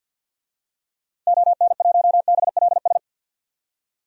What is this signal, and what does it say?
Morse code, heard as a keyed tone, about 50 wpm, 700 Hz. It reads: ON1BLS
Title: DXCC-100 Call Signs - 6 Characters - 50wpm